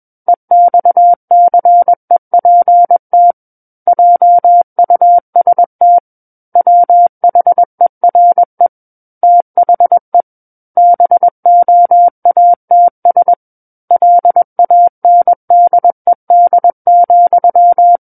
Morse code, heard as a keyed tone, about 21 wpm, 700 Hz. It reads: EXCEPT JUST WHERE THE BOATS LANDED,
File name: Book One / 12-Chapter 12 - 21 WPM